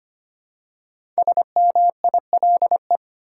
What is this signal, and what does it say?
Morse code, heard as a keyed tone, about 25 wpm, 700 Hz. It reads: SMILE
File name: 5 Letter Words - First 3 Letters Given - 25wpm